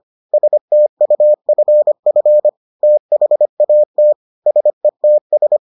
Morse code, heard as a keyed tone, about 25 wpm, 600 Hz. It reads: STUFF THAT SETS